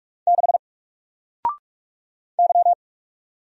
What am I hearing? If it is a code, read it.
Morse code, heard as a keyed tone, about 44 wpm, 700 Hz. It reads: 6 Y
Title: Single Character (Letter:Number:Punctuation:Prosign) 45wpm